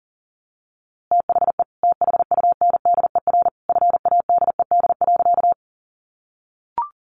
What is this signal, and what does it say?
Morse code, heard as a keyed tone, about 40 wpm, 700 Hz. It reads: THE THUNDER FADED.
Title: A Sound Of Thunder 40wpm